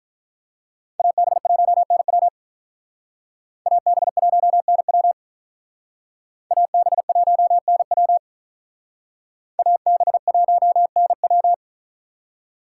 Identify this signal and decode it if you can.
Morse code, heard as a keyed tone, about 45 wpm, 700 Hz. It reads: AB1NW AB1NW AB1NW AB1NW